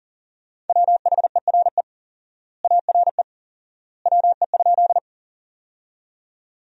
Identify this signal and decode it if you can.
Morse code, heard as a keyed tone, about 37 wpm, 700 Hz. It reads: WHERE ARE WE?
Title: 4x Word-Spacing - Sentences from Top 200 Words - 40wpm